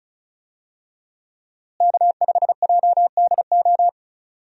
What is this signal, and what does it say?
Morse code, heard as a keyed tone, about 34 wpm, 700 Hz. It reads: K5JDO
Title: USA Call Signs - 5 Characters - Repeated 3x - 35wpm